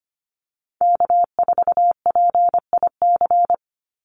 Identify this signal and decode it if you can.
Morse code, heard as a keyed tone, about 25 wpm, 700 Hz. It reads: K4PIC
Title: NA SOTA Chaser Callsigns 25wpm